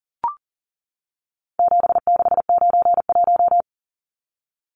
Morse code, heard as a keyed tone, about 40 wpm, 700 Hz. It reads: 7691
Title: Numbers - 4-Digit 40wpm